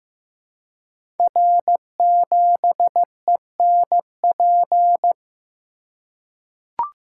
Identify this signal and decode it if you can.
Morse code, heard as a keyed tone, about 15 wpm, 700 Hz. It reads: R7ENP
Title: Code Groups - 5 Letter:Numbers - 15wpm